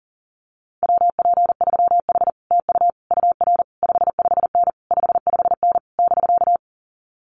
Morse code, heard as a keyed tone, about 40 wpm, 700 Hz. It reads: WP3H TU UR 55N 55N <BK>
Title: Common QSO Phrases Encore 40wpm